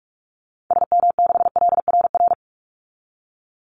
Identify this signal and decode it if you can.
Morse code, heard as a keyed tone, about 44 wpm, 700 Hz. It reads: SM6LRR
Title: Call Signs Repeated 2x - 45wpm